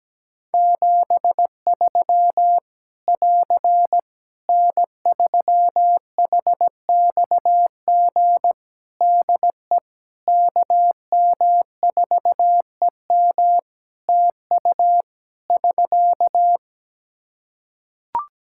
Morse code, heard as a keyed tone, about 17 wpm, 700 Hz. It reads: 73 <AR> N3HXG DE KM4EM TU <SK>
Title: Practice QSO - Exchange 4 - Ending - 17wpm